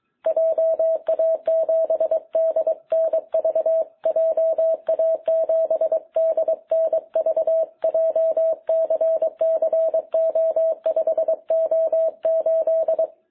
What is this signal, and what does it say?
Morse code, heard as a keyed tone, about 22 wpm, 660 Hz. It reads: JA7DNV JA7DNV JCCO5O8
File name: jh8cbh / 2021cw